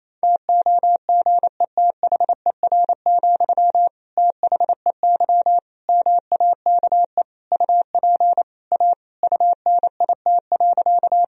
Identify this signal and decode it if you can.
Morse code, heard as a keyed tone, about 28 wpm, 700 Hz. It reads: TOGETHER, THEY MAKE UP A UNIT.
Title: Sentences from Top 1000 Words - Review Part 2 - 28wpm